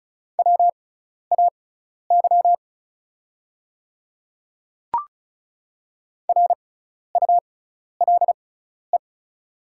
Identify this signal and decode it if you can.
Morse code, heard as a keyed tone, about 36 wpm, spaced as for 12 wpm, 690 Hz. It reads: WAY RULE
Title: 6x Character Spacing - Top 500 Words - 35wpm